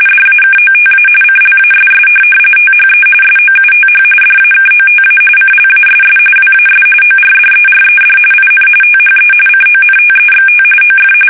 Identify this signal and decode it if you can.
RTTY, about 45.45 baud. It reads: DAKOTA CLASS THE SOUTH DAKOTA CLASS WAS ORDERED IN THE CONTEXT OF
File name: fsk850 / USS-Massachusetts